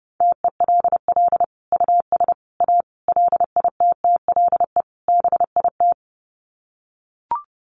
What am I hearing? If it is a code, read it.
Morse code, heard as a keyed tone, about 30 wpm, 700 Hz. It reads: TELL US A LITTLE BIT
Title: Sets of 5 Words Encore2 30wpm